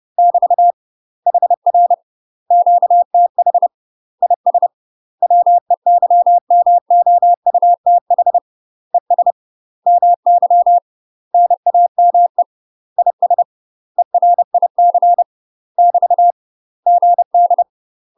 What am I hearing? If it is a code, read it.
Morse code, heard as a keyed tone, about 30 wpm, 700 Hz. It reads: <BT> HR QTH IS WEYMOUTH ES MY NAME IS ERIC <BT> GD
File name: UK MASTER QSOs TRAINING / 30WPM / UK MASTER QSOs 0050